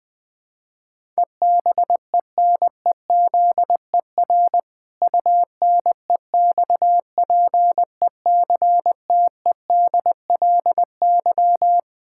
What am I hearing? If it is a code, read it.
Morse code, heard as a keyed tone, about 20 wpm, 700 Hz. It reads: EBENEZER UNEXPECTEDLY
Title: Pangram Challenge v2 - 20wpm